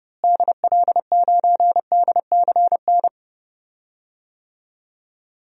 Morse code, heard as a keyed tone, about 30 wpm, 700 Hz. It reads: DL9DCN
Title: DXCC-100 Call Signs - 6 Characters - Repeated 3x - 30wpm